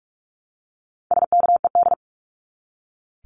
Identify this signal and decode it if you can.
Morse code, heard as a keyed tone, about 44 wpm, 700 Hz. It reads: SKED
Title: Abbreviations 45wpm